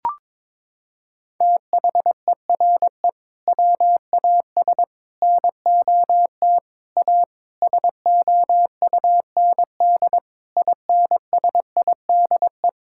Morse code, heard as a keyed tone, about 22 wpm, 700 Hz. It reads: THERE WAS NOT A SOUND INSIDE
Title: Sentences from Top 1000 Words - Review Part 6 - 22wpm